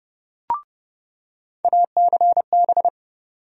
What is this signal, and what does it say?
Morse code, heard as a keyed tone, about 32 wpm, 700 Hz. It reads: ACB
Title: USA Call Sign Postfixes - 3 Letters 30wpm